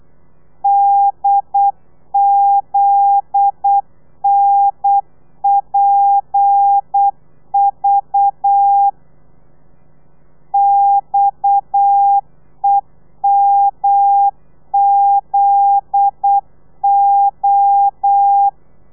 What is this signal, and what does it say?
Morse code, heard as a keyed tone, about 8 wpm, 800 Hz. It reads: DZNPV XEMZO
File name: telegrafia / i2tfj / Lez9 35